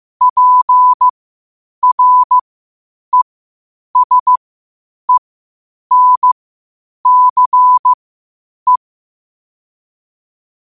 Morse code, heard as a keyed tone, about 15 wpm, spaced as for 8 wpm, 1000 Hz. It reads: PRESENCE
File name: qotdfiles / 8wpm